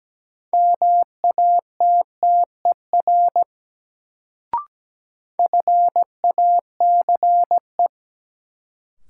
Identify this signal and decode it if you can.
Morse code, heard as a keyed tone, about 17 wpm, 700 Hz. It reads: MATTER FACE
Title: Sets of 3 Words - New Word Drill - 17wpm